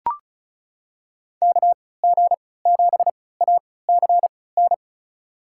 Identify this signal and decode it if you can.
Morse code, heard as a keyed tone, about 34 wpm, spaced as for 20 wpm, 700 Hz. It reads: KG7ACN
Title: USA Call Signs - 6 Characters - 3x Character Spacing - 35wpm